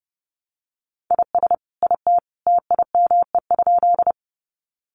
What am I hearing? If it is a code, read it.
Morse code, heard as a keyed tone, about 30 wpm, 700 Hz. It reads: IS IT TIME?
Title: Sentences from Top 100 Words 30wpm